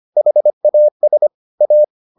Morse code, heard as a keyed tone, about 25 wpm, 600 Hz. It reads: HAS A